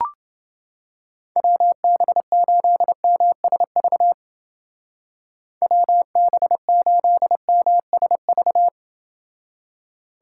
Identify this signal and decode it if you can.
Morse code, heard as a keyed tone, about 29 wpm, 700 Hz. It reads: WB8MSV WB8MSV